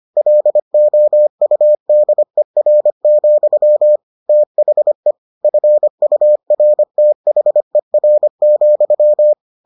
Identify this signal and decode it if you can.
Morse code, heard as a keyed tone, about 25 wpm, 600 Hz. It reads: LOUDER, THE FURTHER,